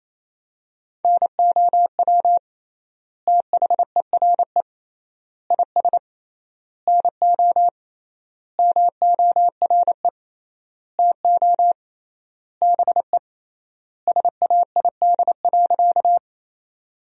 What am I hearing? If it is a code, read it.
Morse code, heard as a keyed tone, about 28 wpm, 700 Hz. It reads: NOW THERE IS NO MORE TO BE SAID.
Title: 3x Word-Spacing - Sentences from Top 100 Words - 28wpm